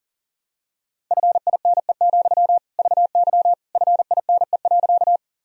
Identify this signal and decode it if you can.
Morse code, heard as a keyed tone, about 37 wpm, 700 Hz. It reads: FINE, VY FINE.
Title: Sentences from Top 500 Words and Abbreviations 40wpm